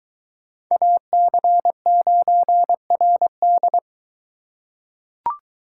Morse code, heard as a keyed tone, about 21 wpm, 700 Hz. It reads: AC9RD